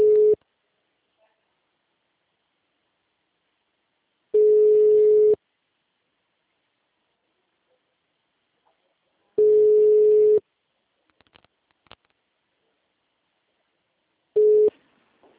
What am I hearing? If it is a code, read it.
Ringback tone (Europe, 425 Hz).